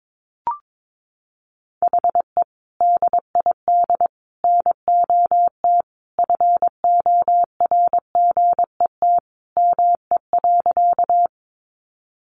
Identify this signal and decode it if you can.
Morse code, heard as a keyed tone, about 22 wpm, 700 Hz. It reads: HE DID NOT FORGET ME.
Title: Sentences from Top 1200 Words 22wpm